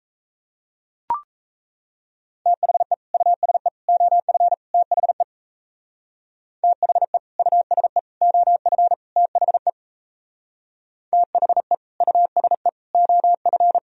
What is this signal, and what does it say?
Morse code, heard as a keyed tone, about 33 wpm, 700 Hz. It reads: THE USE OF THE THE USE OF THE THE USE OF
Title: Speed-Racing - Sets of 4 Words 28wpm